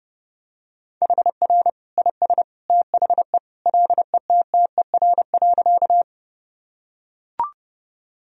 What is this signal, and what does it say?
Morse code, heard as a keyed tone, about 30 wpm, 700 Hz. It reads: HR IS THE LETTER.
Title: Sentences from Top 300 Words and Abbreviations 30wpm